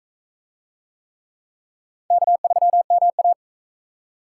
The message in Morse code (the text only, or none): K3MA